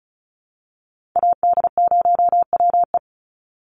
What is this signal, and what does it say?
Morse code, heard as a keyed tone, about 34 wpm, 700 Hz. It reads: AD0WE
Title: Call Signs Repeated 2x - 35wpm